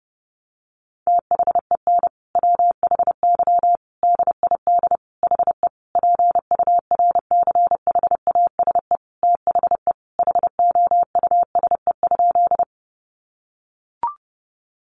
Morse code, heard as a keyed tone, about 30 wpm, 700 Hz. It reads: THEN WHY DID HE PURCHASE THE HOUSE?
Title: Sentences from Top 800 Words 30wpm